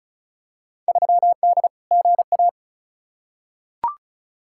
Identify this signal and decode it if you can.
Morse code, heard as a keyed tone, about 36 wpm, 690 Hz. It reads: 3D GA
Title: Field Day Exchanges Encore - 35wpm